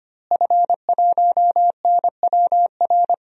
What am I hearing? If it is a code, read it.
Morse code, heard as a keyed tone, about 25 wpm, 700 Hz. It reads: F1NWR
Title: DXCC-100 Call Signs - 5 Characters - Repeated 2x - 25wpm